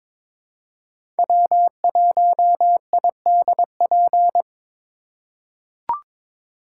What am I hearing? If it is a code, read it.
Morse code, heard as a keyed tone, about 22 wpm, 700 Hz. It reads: W1IDP
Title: USA Call Signs - 2x Character Spacing - 22wpm